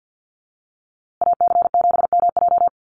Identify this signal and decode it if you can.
Morse code, heard as a keyed tone, about 50 wpm, 700 Hz. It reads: AC7MJ
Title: USA Call Signs Repeated 2x 50wpm